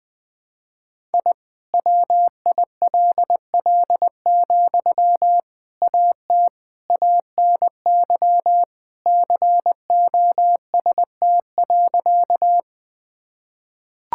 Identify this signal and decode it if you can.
Morse code, heard as a keyed tone, about 20 wpm, 700 Hz. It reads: I WILL, AT ANY COST.
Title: Sentences from Top 1000 Words - Review Part 5 - 20wpm